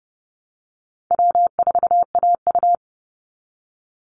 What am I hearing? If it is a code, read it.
Morse code, heard as a keyed tone, about 30 wpm, 700 Hz. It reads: W4AU
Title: Easy Call Signs Repeated 3x - 30wpm